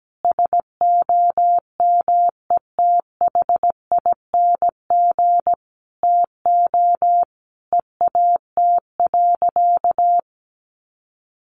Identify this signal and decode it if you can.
Morse code, heard as a keyed tone, about 17 wpm, 700 Hz. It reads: SOMETHING TO EAT.
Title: Sentences from Top 600 Words 17wpm